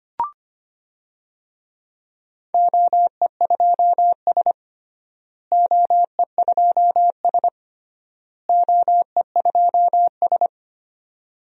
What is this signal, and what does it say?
Morse code, heard as a keyed tone, about 25 wpm, 700 Hz. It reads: OE2S OE2S OE2S
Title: DXCC-100 Call Signs - 4 Characters - Repeated 3x - 25wpm